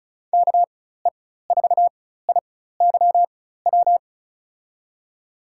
Morse code, heard as a keyed tone, about 36 wpm, spaced as for 16 wpm, 700 Hz. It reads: KE4IYW
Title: USA Call Signs - 6 Characters - 4x Character Spacing - 35wpm